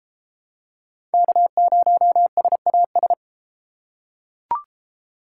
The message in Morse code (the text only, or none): K0SAS